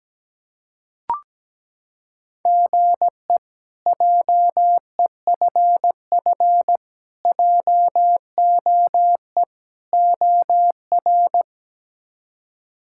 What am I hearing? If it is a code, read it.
Morse code, heard as a keyed tone, about 17 wpm, 700 Hz. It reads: GE JEFF JOE OR
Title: K1USN SST QSO Practice - 17wpm